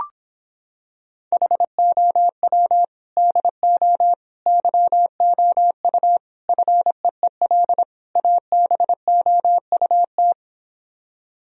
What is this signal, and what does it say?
Morse code, heard as a keyed tone, about 26 wpm, 700 Hz. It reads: HOW DO YOU FEEL ABOUT